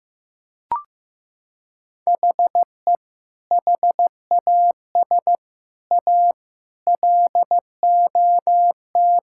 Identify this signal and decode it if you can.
Morse code, heard as a keyed tone, about 15 wpm, 700 Hz. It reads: HE HAS A LOT
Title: Sentences from Top 1000 Words - Review Part 6 - 15wpm